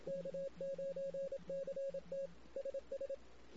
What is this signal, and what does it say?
Morse code, heard as a keyed tone, about 27 wpm, 550 Hz. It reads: K9CT SS
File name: CW / 2008-ss-cw-k5tr-op-n5rz -16:00